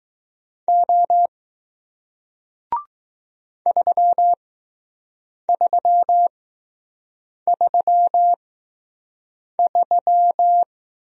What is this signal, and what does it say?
Morse code, heard as a keyed tone, about 20 wpm, spaced as for 6 wpm, 700 Hz. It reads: O 3333